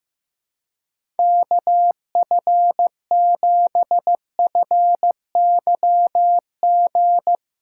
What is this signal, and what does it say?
Morse code, heard as a keyed tone, about 15 wpm, 700 Hz. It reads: KF7FYG
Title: USA Callsign DE Callsign 15wpm